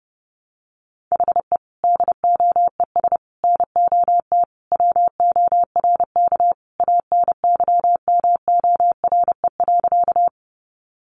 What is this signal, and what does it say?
Morse code, heard as a keyed tone, about 30 wpm, 700 Hz. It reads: HE DOES NOT WORK ANYMORE.
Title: Additional Easy Phrases 30wpm